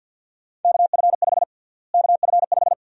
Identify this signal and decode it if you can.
Morse code, heard as a keyed tone, about 53 wpm, 700 Hz. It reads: KF5 KF5
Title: USA Call Sign Prefixes - Repeated 2x - 50wpm